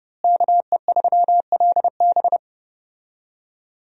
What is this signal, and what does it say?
Morse code, heard as a keyed tone, about 28 wpm, 700 Hz. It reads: KE3LB
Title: USA Call Signs - 5 Characters - Repeated 3x - 30wpm